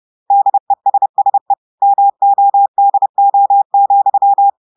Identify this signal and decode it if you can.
Morse code, heard as a keyed tone, about 30 wpm, 800 Hz. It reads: DESSE MODO,
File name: portugais texte en clair 2 vitesse 30 mots minutes